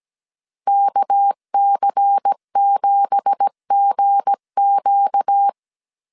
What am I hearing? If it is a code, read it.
Morse code, heard as a keyed tone, about 17 wpm, 800 Hz. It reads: KC7GQ
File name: CW / DET / 1 9 1500C S C 16wpm 2909